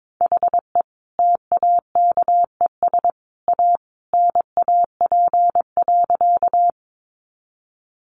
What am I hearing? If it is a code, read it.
Morse code, heard as a keyed tone, about 22 wpm, 700 Hz. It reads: HE TAKES A NAP.
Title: Easy Phrases 22wpm